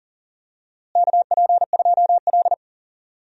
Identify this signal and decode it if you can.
Morse code, heard as a keyed tone, about 39 wpm, 700 Hz. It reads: KP2L
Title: DXCC-100 Call Signs - 4 Characters - Repeated 3x - 40wpm